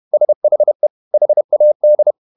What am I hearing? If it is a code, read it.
Morse code, heard as a keyed tone, about 31 wpm, 600 Hz. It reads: SHE HAD
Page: Never 31wpm | CW med Gnister